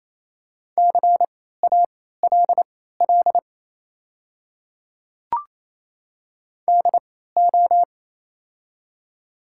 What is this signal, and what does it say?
Morse code, heard as a keyed tone, about 27 wpm, spaced as for 16 wpm, 700 Hz. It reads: CALL DO
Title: 3x Character Spacing - Top 100 Words - 28wpm